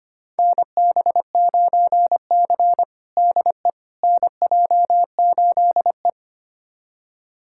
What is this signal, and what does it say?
Morse code, heard as a keyed tone, about 25 wpm, 700 Hz. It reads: NB9C DE NJ8E
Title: USA Callsign DE Callsign - 4 Characters - 25wpm